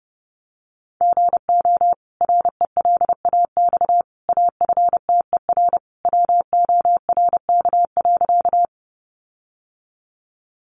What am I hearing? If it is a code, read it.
Morse code, heard as a keyed tone, about 30 wpm, 700 Hz. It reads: GO RELAX AFTER WORK.
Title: Sentences from Top 1400 Words 30wpm